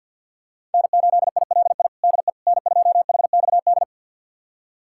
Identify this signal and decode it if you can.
Morse code, heard as a keyed tone, about 52 wpm, 690 Hz. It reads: N8ILI DE N2HXD
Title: USA Callsign DE Callsign - 5 Characters - 50wpm